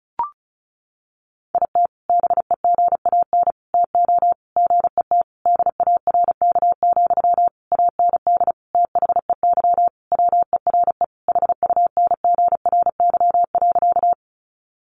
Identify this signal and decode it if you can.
Morse code, heard as a keyed tone, about 34 wpm, 700 Hz. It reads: IT BEGAN TO GET DARK, AND THEY WERE HUNGRY.
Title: Sentences from Top 1400 Words 35wpm